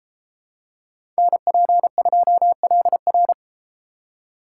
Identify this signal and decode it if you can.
Morse code, heard as a keyed tone, about 31 wpm, 700 Hz. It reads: NP2LR